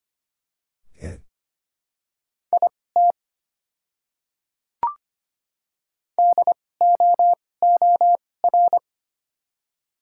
Morse code, heard as a keyed tone, about 25 wpm, spaced as for 18 wpm, 690 Hz. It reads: IT DOOR
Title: 2x Character Spacing - Top 300 Words - 25wpm